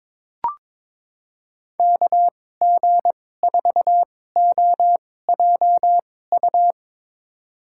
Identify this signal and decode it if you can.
Morse code, heard as a keyed tone, about 22 wpm, spaced as for 16 wpm, 700 Hz. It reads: KG4OJU
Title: USA Call Signs - 6 Characters - 2x Character Spacing - 22wpm